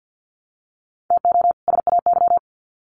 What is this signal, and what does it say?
Morse code, heard as a keyed tone, about 50 wpm, 700 Hz. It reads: TO SAY
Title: Sets of 2 Words 50wpm